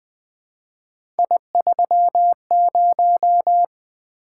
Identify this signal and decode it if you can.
Morse code, heard as a keyed tone, about 21 wpm, 700 Hz. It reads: I30
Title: Three Letter-Number Combinations - 20wpm